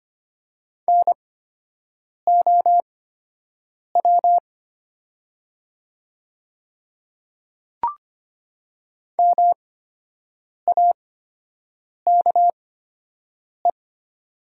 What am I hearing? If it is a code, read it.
Morse code, heard as a keyed tone, about 25 wpm, spaced as for 7 wpm, 700 Hz. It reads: NOW MAKE
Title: 8x Character Spacing - Top 200 Words - 25wpm